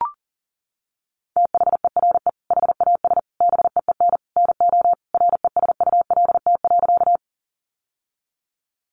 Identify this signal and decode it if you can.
Morse code, heard as a keyed tone, about 40 wpm, 700 Hz. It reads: THERE HAS BEEN NO RESULT.
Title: Sentences from Top 700 Words 40wpm